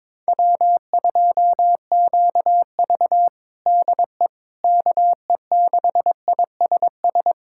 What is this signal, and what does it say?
Morse code, heard as a keyed tone, about 22 wpm, 700 Hz. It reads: W2QV DE KE6ISS
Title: USA Callsign DE Callsign 22wpm